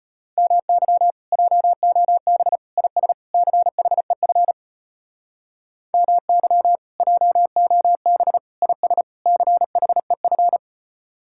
Morse code, heard as a keyed tone, about 33 wpm, 700 Hz. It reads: MY JOB IS CHEF MY JOB IS CHEF